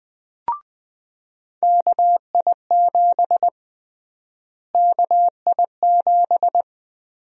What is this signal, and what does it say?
Morse code, heard as a keyed tone, about 21 wpm, 700 Hz. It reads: KI7 KI7
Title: USA Call Sign Prefixes - Repeated 2x - 20wpm